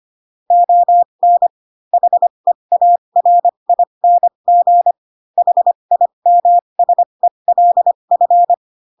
Morse code, heard as a keyed tone, about 25 wpm, spaced as for 22 wpm, 700 Hz. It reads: ON HEARING HIMSELF
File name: Pinocchio / Pinocchio ch2 18 25 ik1hge